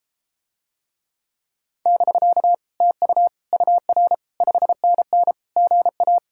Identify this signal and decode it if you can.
Morse code, heard as a keyed tone, about 33 wpm, 700 Hz. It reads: <BK> TU UR 5NN GA